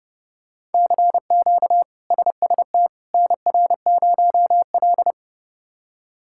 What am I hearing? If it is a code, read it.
Morse code, heard as a keyed tone, about 30 wpm, 700 Hz. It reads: CQ SST NR0L
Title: K1USN SST QSO Practice - 30wpm